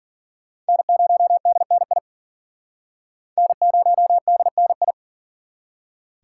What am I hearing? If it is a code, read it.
Morse code, heard as a keyed tone, about 54 wpm, 700 Hz. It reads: N0DNI N0DNI